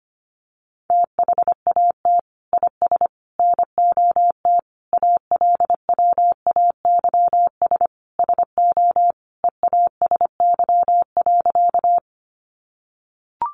Morse code, heard as a keyed tone, about 25 wpm, 700 Hz. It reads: THAT IS NOT ALWAYS SO EASY.
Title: Sentences from Top 700 Words 25wpm